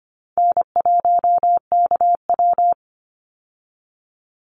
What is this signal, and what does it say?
Morse code, heard as a keyed tone, about 25 wpm, 700 Hz. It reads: N1KW
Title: Easy Call Signs Repeated 3x - 25wpm